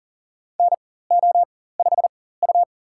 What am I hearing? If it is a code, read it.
Morse code, heard as a keyed tone, about 37 wpm, spaced as for 18 wpm, 700 Hz. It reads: NO5U
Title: USA Call Signs - 4 Characters - 4x Character Spacing - 40wpm